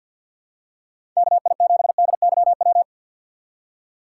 Morse code, heard as a keyed tone, about 53 wpm, 700 Hz. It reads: KI7DYW
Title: USA Call Signs - 4x Character Spacing - 50wpm